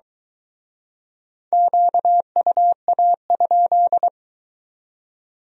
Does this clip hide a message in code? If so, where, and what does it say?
Yes: Morse code from 1.5 to 4.1 s, heard as a keyed tone, about 24 wpm, 700 Hz. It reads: QUA?